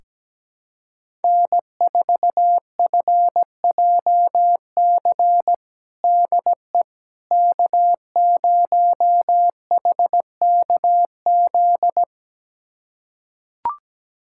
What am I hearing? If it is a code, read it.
Morse code, heard as a keyed tone, about 17 wpm, 700 Hz. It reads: N4FJC DE K0HKZ
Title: USA Callsign DE Callsign - 5 Characters - 17wpm